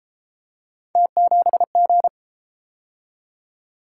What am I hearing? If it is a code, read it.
Morse code, heard as a keyed tone, about 31 wpm, 700 Hz. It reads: T7G